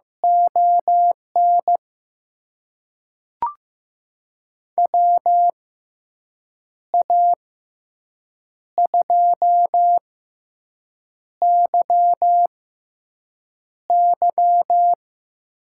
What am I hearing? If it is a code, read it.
Morse code, heard as a keyed tone, about 15 wpm, spaced as for 6 wpm, 700 Hz. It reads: ON WA2YY
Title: USA Call Signs - 6 Characters - 6x Character Spacing - 15wpm